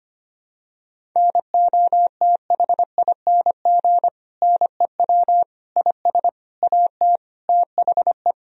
Morse code, heard as a keyed tone, about 25 wpm, 700 Hz. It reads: NOTHING NEW IS AT THE
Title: Sentences from Top 1000 Words - Review Part 1 - 25wpm